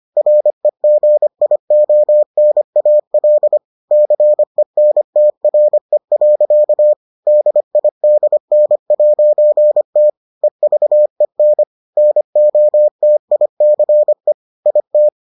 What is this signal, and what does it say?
Morse code, heard as a keyed tone, about 25 wpm, 600 Hz. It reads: REGIONAL CENTRE. DIDN'T EVEN NOTICE IT